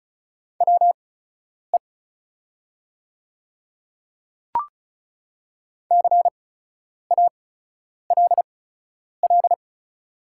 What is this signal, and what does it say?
Morse code, heard as a keyed tone, about 36 wpm, spaced as for 10 wpm, 700 Hz. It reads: WE CALL
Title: 8x Character Spacing - Top 300 Words - 35wpm